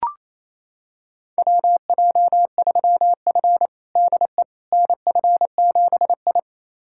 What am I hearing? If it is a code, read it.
Morse code, heard as a keyed tone, about 28 wpm, 700 Hz. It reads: WJ3F DE NF7I
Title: USA Callsign DE Callsign - 4 Characters - 28wpm